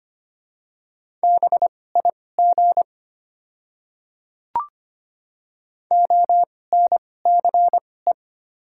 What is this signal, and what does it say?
Morse code, heard as a keyed tone, about 25 wpm, spaced as for 18 wpm, 700 Hz. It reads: BIG ONCE
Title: 2x Character Spacing - Top 300 Words - 25wpm